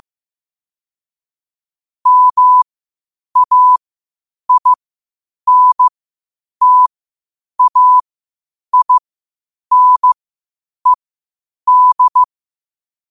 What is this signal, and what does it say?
Morse code, heard as a keyed tone, about 15 wpm, spaced as for 8 wpm, 1000 Hz. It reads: MAINTAINED
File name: qotdfiles / 8wpm